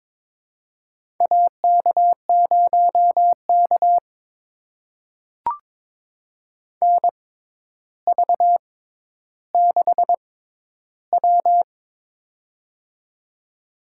Morse code, heard as a keyed tone, about 22 wpm, spaced as for 11 wpm, 700 Hz. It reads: AK0K NV6W
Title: USA Call Signs - 4 Characters - 6x Character Spacing - 22wpm